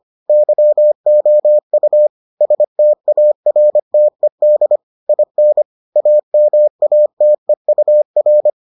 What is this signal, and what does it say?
Morse code, heard as a keyed tone, about 25 wpm, 600 Hz. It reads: YOU STARTED IN AMATEUR